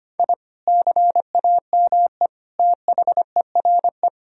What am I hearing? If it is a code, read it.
Morse code, heard as a keyed tone, about 25 wpm, 700 Hz. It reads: I CAME THERE
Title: Sentences from Top 1000 Words - Review Part 4 - 25wpm